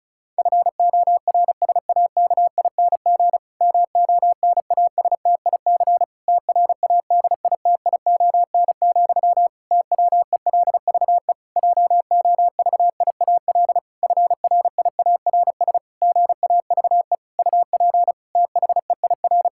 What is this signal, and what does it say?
Morse code, heard as a keyed tone, about 36 wpm, 700 Hz. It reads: FORSAKING MONASTIC TRADITION, TWELVE JOVIAL FRIARS GAVE UP THEIR
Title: Pangram Challenge v2 - 35wpm